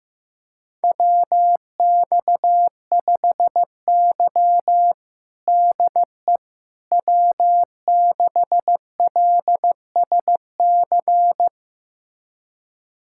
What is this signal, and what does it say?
Morse code, heard as a keyed tone, about 15 wpm, 700 Hz. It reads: WX5Y DE W6LSC
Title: USA Callsign DE Callsign 15wpm